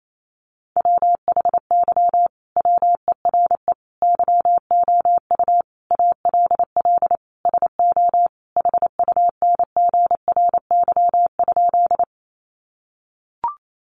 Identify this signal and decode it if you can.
Morse code, heard as a keyed tone, about 28 wpm, 700 Hz. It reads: WHY WERE YOU ALL SO HUNGRY?
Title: Sentences from Top 1400 Words 28wpm